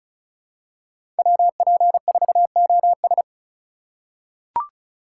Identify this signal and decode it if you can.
Morse code, heard as a keyed tone, about 35 wpm, 700 Hz. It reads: WP4OS
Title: USA Call Signs - 5 Characters - 2x Character Spacing - 35wpm